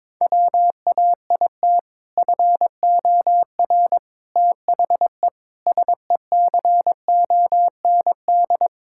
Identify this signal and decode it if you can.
Morse code, heard as a keyed tone, about 22 wpm, 700 Hz. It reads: WAIT FOR THE SECOND
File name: practice / 22wpm / Sentences with 2-Word Phrases - 22wpm